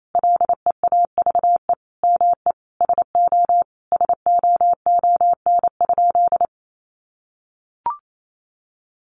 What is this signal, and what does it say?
Morse code, heard as a keyed tone, about 28 wpm, 700 Hz. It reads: LEAVE ME SO SOON?
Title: Sentences from Top 400 Words 28wpm